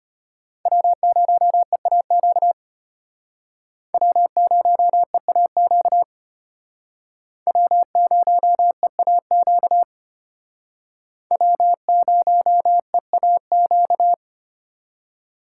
Morse code, heard as a keyed tone, about 32 wpm, 700 Hz. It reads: W0EAQ W0EAQ W0EAQ W0EAQ